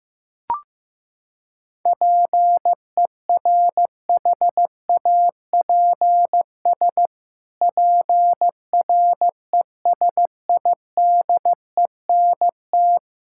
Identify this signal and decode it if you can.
Morse code, heard as a keyed tone, about 15 wpm, 700 Hz. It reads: PERHAPS PRESIDENT
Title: Pangram Challenge v2 - 15wpm